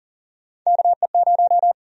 Morse code, heard as a keyed tone, about 40 wpm, 700 Hz. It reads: KE0
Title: USA Call Sign Prefixes - Repeated 2x - 40wpm